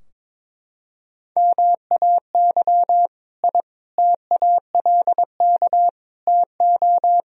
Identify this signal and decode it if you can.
Morse code, heard as a keyed tone, about 22 wpm, 700 Hz. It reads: MAY I TALK TO
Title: Sentences from Top 1000 Words - Review Part 4 - 22wpm